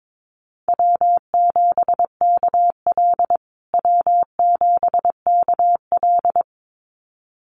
Morse code, heard as a keyed tone, about 22 wpm, 700 Hz. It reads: W7KL W7KL
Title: USA Call Signs Repeated 2x 22wpm